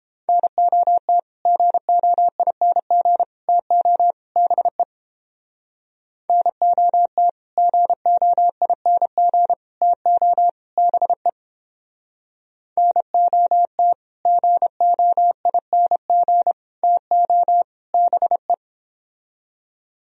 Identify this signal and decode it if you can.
Morse code, heard as a keyed tone, about 31 wpm, 700 Hz. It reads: NOT GOING TO BE NOT GOING TO BE NOT GOING TO BE